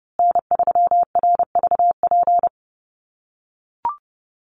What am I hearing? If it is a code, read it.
Morse code, heard as a keyed tone, about 29 wpm, 700 Hz. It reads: N3RVP
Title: USA Call Signs Repeated 2x 30wpm